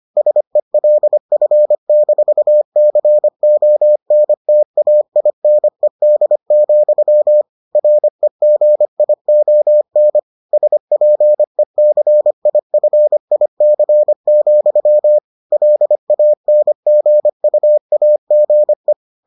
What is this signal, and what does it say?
Morse code, heard as a keyed tone, about 25 wpm, 600 Hz. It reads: SELF-CONTAINED, REGION SPECIFIC, LANGUAGE